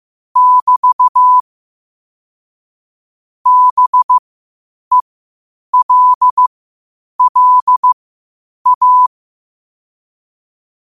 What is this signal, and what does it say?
Morse code, heard as a keyed tone, about 15 wpm, spaced as for 8 wpm, 1000 Hz. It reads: <BT> BELLA
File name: qotdfiles / 8wpm